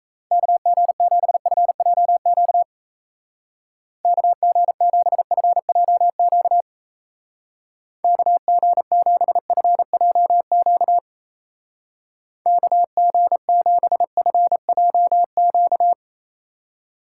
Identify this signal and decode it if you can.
Morse code, heard as a keyed tone, about 34 wpm, 700 Hz. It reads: KG7FJQ KG7FJQ KG7FJQ KG7FJQ
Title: Speed Racing - USA Call Signs 28wpm